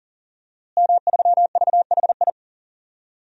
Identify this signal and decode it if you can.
Morse code, heard as a keyed tone, about 40 wpm, 700 Hz. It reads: M3VHI
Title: DXCC-100 Call Signs - 5 Characters - Repeated 2x - 40wpm